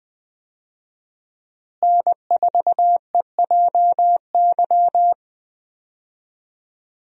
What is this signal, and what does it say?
Morse code, heard as a keyed tone, about 20 wpm, 700 Hz. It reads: N4EJY
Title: USA Call Signs - 5 Characters - Repeated 3x - 20wpm